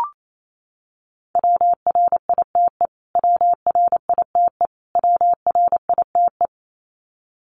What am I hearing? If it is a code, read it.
Morse code, heard as a keyed tone, about 28 wpm, 700 Hz. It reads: WRITE WRITE WRITE
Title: Top 500 Words Repeated 28wpm